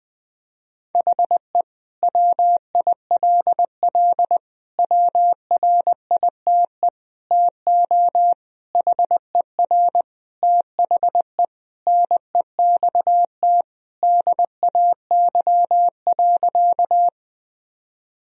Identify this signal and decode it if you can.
Morse code, heard as a keyed tone, about 21 wpm, 700 Hz. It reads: HE WILL WRITE TO HER THE NEXT DAY.
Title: Sentences from Top 1000 Words - Review Part 1 - 20wpm